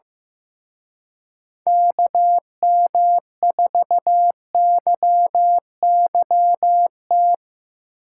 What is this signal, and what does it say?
Morse code, heard as a keyed tone, about 17 wpm, 700 Hz. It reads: KM4YYT
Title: Speed Racing - USA 2x3 Call Signs 15wpm